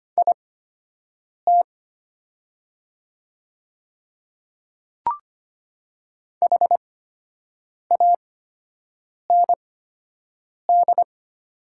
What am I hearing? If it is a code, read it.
Morse code, heard as a keyed tone, about 25 wpm, spaced as for 7 wpm, 700 Hz. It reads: IT HAND
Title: 8x Character Spacing - Top 300 Words - 25wpm